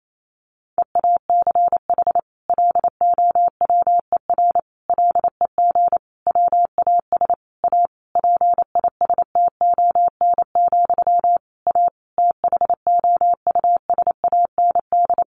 Morse code, heard as a keyed tone, about 28 wpm, 700 Hz. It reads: EACH LOWER LEG WAS A PISTON, A THOUSAND
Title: A Sound Of Thunder 28wpm